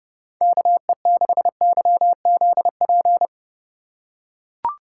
Morse code, heard as a keyed tone, about 30 wpm, 700 Hz. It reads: KE6YZP